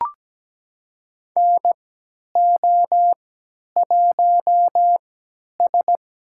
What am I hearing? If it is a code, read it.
Morse code, heard as a keyed tone, about 17 wpm, spaced as for 10 wpm, 700 Hz. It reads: NO1S
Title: USA Call Signs - 4 Characters - 3x Character Spacing - 17wpm